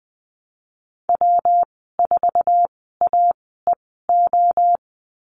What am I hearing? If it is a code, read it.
Morse code, heard as a keyed tone, about 20 wpm, spaced as for 15 wpm, 700 Hz. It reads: W4AEO
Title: 122 Lesson 20g - Number 3 - Call Signs 20wpm